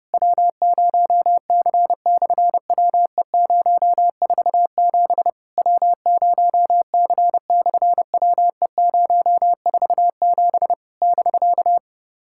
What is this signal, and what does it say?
Morse code, heard as a keyed tone, about 30 wpm, 700 Hz. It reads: W0C/WE047 W0C/WE047 <BK>
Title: SOTA QSO Practice 30wpm